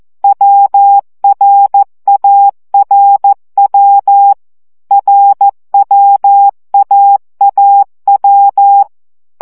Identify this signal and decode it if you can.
Morse code, heard as a keyed tone, about 14 wpm, 800 Hz. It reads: WRARW RWAAW
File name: CW gyakorlas a r w 12 baud